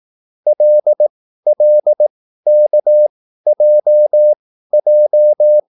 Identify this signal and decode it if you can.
Morse code, heard as a keyed tone, about 18 wpm, spaced as for 13 wpm, 600 Hz. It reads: LLKJJ
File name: QSL / CW lesson